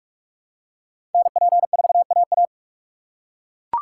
Morse code, heard as a keyed tone, about 40 wpm, 700 Hz. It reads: NP4AA